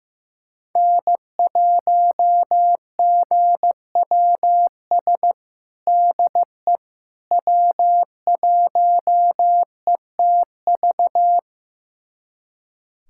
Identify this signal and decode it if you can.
Morse code, heard as a keyed tone, about 16 wpm, 700 Hz. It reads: N1GWS DE W1ETV
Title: USA Callsign DE Callsign - 5 Characters - 15wpm